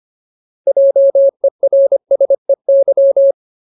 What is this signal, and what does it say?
Morse code, heard as a keyed tone, about 25 wpm, 550 Hz. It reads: JERSEY
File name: Ultimate Word List / 25WPM Ultimate